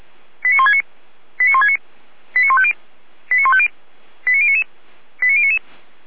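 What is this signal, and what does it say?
Five-tone selective call (ZVEI), heard as quick runs of five tones, tones of about 70 ms. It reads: 89169 89169 89150 89150 89099 89099
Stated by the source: ZVEI five-tone call 89223 received on SDR, from the Feuerwehr Orlamünde (fire brigade) audio server.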